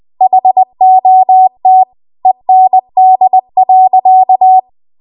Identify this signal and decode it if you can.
Morse code, heard as a keyed tone, about 20 wpm, 750 Hz. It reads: HOT END.
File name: Archive / 20 WPM / 20WPM